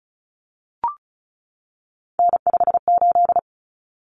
Morse code, heard as a keyed tone, about 34 wpm, 700 Hz. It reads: N58
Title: Common Cut Numbers - 3-Digit 35wpm